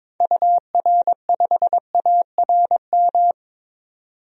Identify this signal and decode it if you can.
Morse code, heard as a keyed tone, about 22 wpm, 700 Hz. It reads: UR5ARM
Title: DXCC-100 Call Signs - 6 Characters - 22wpm